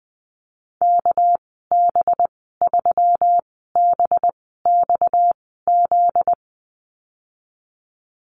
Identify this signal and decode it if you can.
Morse code, heard as a keyed tone, about 20 wpm, spaced as for 14 wpm, 700 Hz. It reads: KB3BXZ
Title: 245 Lesson 39g - Letter z - Call Signs 20wpm